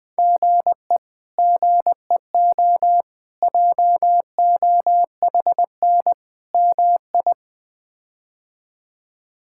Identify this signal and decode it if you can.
Morse code, heard as a keyed tone, about 20 wpm, 700 Hz. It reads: GE GEO JOHN MI
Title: K1USN SST QSO Practice - 20wpm